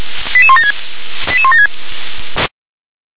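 Five-tone selective call (ZVEI), heard as quick runs of five tones, tones of about 70 ms. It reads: 88176 88176
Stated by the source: ZVEI five-tone selective call received on SDR.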